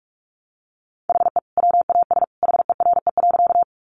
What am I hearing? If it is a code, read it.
Morse code, heard as a keyed tone, about 44 wpm, 700 Hz. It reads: HE WAS HERE.
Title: Sentences from Top 200 Words 45wpm